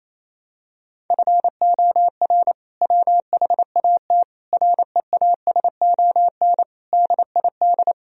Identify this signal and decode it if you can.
Morse code, heard as a keyed tone, about 28 wpm, 700 Hz. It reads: FOR WHAT REASON DID
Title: Sentences from Top 1000 Words - Review Part 4 - 28wpm